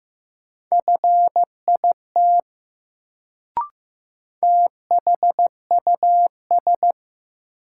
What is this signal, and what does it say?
Morse code, heard as a keyed tone, about 15 wpm, 700 Hz. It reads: FIT THUS
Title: Sentences from Top 800 Words - New Word Drill - 15wpm